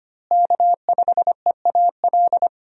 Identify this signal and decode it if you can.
Morse code, heard as a keyed tone, about 25 wpm, 700 Hz. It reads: K5EAL
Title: USA Call Signs - 5 Characters - Repeated 3x - 25wpm